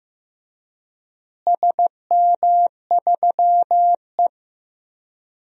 Morse code, heard as a keyed tone, about 16 wpm, 700 Hz. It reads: SM3E
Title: DXCC-100 Call Signs - 4 Characters - Repeated 3x - 15wpm